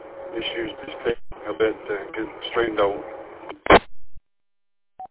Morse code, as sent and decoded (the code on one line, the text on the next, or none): none